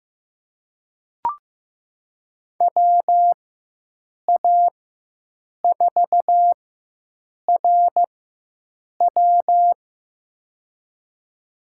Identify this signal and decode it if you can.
Morse code, heard as a keyed tone, about 15 wpm, spaced as for 7 wpm, 700 Hz. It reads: WA4RW
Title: USA Call Signs - 5 Characters - 4x Character Spacing - 15wpm